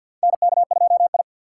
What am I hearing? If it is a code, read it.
Morse code, heard as a keyed tone, about 47 wpm, 700 Hz. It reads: NK2I
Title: USA Call Signs - 4 Characters - 6x Character Spacing - 50wpm